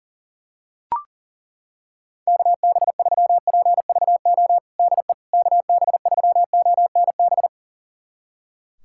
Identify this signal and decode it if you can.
Morse code, heard as a keyed tone, about 40 wpm, 700 Hz. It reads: KB3PVO DE KB3ONB
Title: USA Callsign DE Callsign - 6 Characters - 40wpm